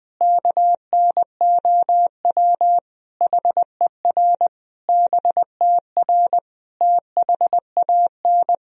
Morse code, heard as a keyed tone, about 19 wpm, 700 Hz. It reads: KNOW HER BTR THAN
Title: Sentences from Top 400 Words and Abbreviations 20wpm